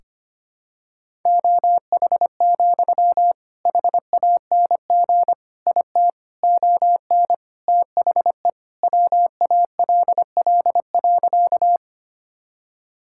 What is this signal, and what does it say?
Morse code, heard as a keyed tone, about 25 wpm, 700 Hz. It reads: OH, HANG IT ON THE WALL.
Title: Sentences from Top 1600 Words 25wpm